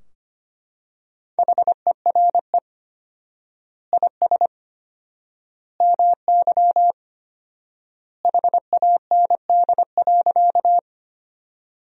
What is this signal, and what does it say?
Morse code, heard as a keyed tone, about 25 wpm, 700 Hz. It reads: HERE IS MY HAND.
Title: 4x Word-Spacing - Sentences from Top 200 Words - 25wpm